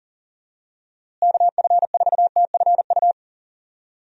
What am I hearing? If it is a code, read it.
Morse code, heard as a keyed tone, about 40 wpm, 700 Hz. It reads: KF4TFU
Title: USA Call Signs - 6 Characters - Repeated 3x - 40wpm